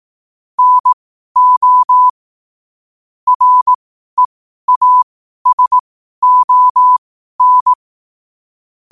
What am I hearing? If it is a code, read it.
Morse code, heard as a keyed tone, about 18 wpm, spaced as for 12 wpm, 1000 Hz. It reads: NO REASON
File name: qotdfiles / 12wpm